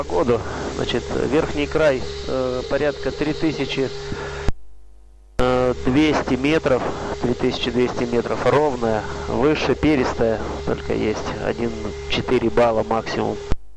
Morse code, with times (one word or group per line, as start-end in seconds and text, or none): none